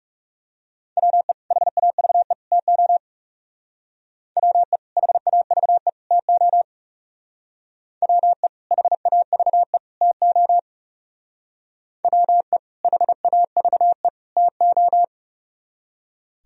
Morse code, heard as a keyed tone, about 37 wpm, 700 Hz. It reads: WE HAVE TO WE HAVE TO WE HAVE TO WE HAVE TO
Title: Speed-Racing - Sets of 3 Words 30wpm